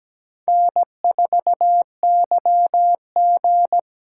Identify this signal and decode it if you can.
Morse code, heard as a keyed tone, about 17 wpm, 700 Hz. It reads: N4YG
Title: USA Call Signs - 4x Character Spacing - 17wpm